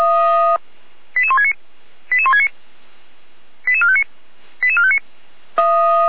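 Five-tone selective call (ZVEI), heard as quick runs of five tones, tones of about 70 ms. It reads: 88169 88169 88459 88459
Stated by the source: Five-tone ZVEI selective call received on an SDR, channel 455.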